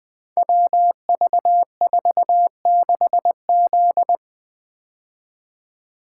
Morse code, heard as a keyed tone, about 20 wpm, 700 Hz. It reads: WV46Z
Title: Code Groups - 5 Letter:Numbers - 20wpm